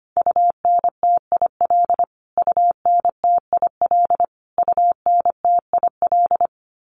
Morse code, heard as a keyed tone, about 25 wpm, 700 Hz. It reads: UNTIL UNTIL UNTIL
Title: Top 300 Words Repeated 25wpm